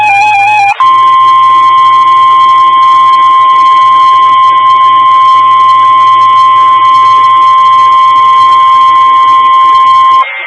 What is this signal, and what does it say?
Emergency Alert System (SAME) tones (the 1050 Hz attention tone). It is heard from 0.8 to 10.2 s.
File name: W5NGU Denton County ARC and Skywarn 146920 Mhz Repeater